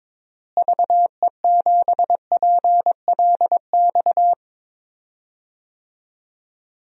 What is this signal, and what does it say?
Morse code, heard as a keyed tone, about 22 wpm, 700 Hz. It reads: VE7PLX
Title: DXCC-100 Call Signs - 6 Characters - Repeated 2x - 22wpm